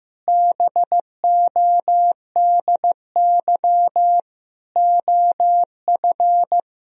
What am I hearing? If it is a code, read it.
Morse code, heard as a keyed tone, about 15 wpm, 700 Hz. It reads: BODY OF
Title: Sentences from Top 1000 Words - Review Part 4 - 15wpm